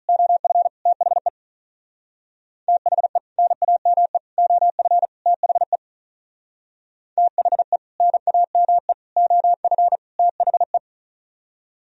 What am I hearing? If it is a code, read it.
Morse code, heard as a keyed tone, about 44 wpm, 700 Hz. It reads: OF THE THE NAME OF THE THE NAME OF THE
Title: Speed-Racing - Sets of 4 Words 35wpm